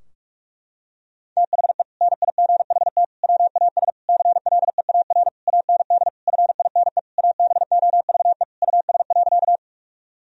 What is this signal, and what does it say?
Morse code, heard as a keyed tone, about 46 wpm, 700 Hz. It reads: THE NIGHT WAS CLEAR AND FINE ABOVE US.
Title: Sentences from Top 1000 Words - Review Part 2 - 45wpm